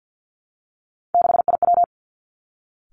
Morse code, heard as a keyed tone, about 50 wpm, 700 Hz. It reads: 6IW
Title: Three Challenging Letter-Number Combinations 50wpm